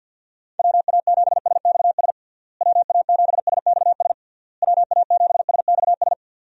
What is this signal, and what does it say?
Morse code, heard as a keyed tone, about 52 wpm, 700 Hz. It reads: WA7SXS WA7SXS WA7SXS
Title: USA Call Signs - 6 Characters - Repeated 3x - 50wpm